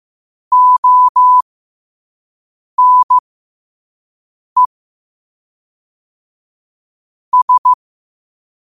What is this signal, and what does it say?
Morse code, heard as a keyed tone, about 15 wpm, spaced as for 5 wpm, 1000 Hz. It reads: ONE S
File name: qotdfiles / 5wpm